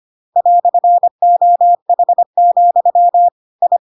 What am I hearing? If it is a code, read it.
Morse code, heard as a keyed tone, about 25 wpm, 700 Hz. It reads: "OH, I
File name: Pinocchio / Pinocchio ch1 25 25 ik1hge